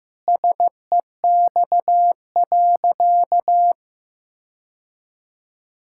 Morse code, heard as a keyed tone, about 16 wpm, 700 Hz. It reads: SEX.
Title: Sentences from Top 1600 Words 15wpm